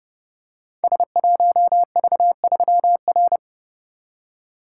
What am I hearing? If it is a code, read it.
Morse code, heard as a keyed tone, about 29 wpm, 700 Hz. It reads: S1V3R
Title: Code Groups - 5 Letter:Numbers - 30wpm